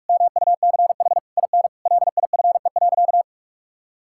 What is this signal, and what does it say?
Morse code, heard as a keyed tone, about 46 wpm, 700 Hz. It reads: MUCH IN LIFE.
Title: Sentences from Top 1000 Words - Review Part 6 - 45wpm